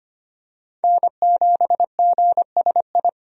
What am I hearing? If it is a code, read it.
Morse code, heard as a keyed tone, about 25 wpm, 700 Hz. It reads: N7GSI
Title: USA Call Signs - 5 Characters - 25wpm